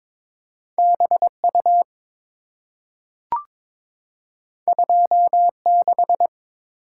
Morse code, heard as a keyed tone, about 23 wpm, 690 Hz. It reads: BU 26
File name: practice / 22wpm / Two Letter-Number Combinations - 22wpm